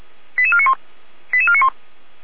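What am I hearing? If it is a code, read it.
Five-tone selective call (ZVEI), heard as quick runs of five tones, tones of about 80 ms. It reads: 88491 88491
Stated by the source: ZVEI five-tone selective call received on an SDR.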